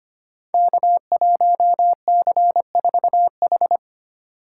Morse code, heard as a keyed tone, about 25 wpm, 700 Hz. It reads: K1C4H
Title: Code Groups - 5 Letter:Numbers - 25wpm